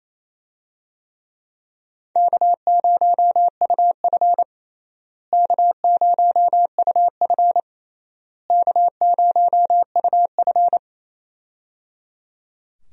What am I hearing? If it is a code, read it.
Morse code, heard as a keyed tone, about 27 wpm, 700 Hz. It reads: K0UF K0UF K0UF
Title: USA Call Signs - 4 Characters - Repeated 3x - 28wpm